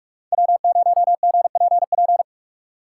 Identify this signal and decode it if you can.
Morse code, heard as a keyed tone, about 46 wpm, 700 Hz. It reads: W0GPP
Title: USA Call Signs - 5 Characters - 4x Character Spacing - 45wpm